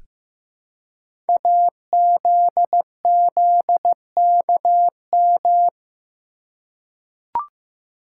Morse code, heard as a keyed tone, about 15 wpm, 700 Hz. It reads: AZZKM
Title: Code Groups - 5 Letters - 15wpm